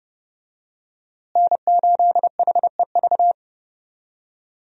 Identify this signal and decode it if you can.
Morse code, heard as a keyed tone, about 31 wpm, 700 Hz. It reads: N8HEV
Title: USA Call Signs - 5 Characters - Repeated 3x - 30wpm